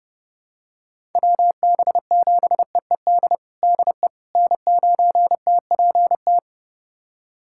Morse code, heard as a keyed tone, about 30 wpm, 700 Hz. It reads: WB7EED DE N9TPT
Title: USA Callsign DE Callsign 30wpm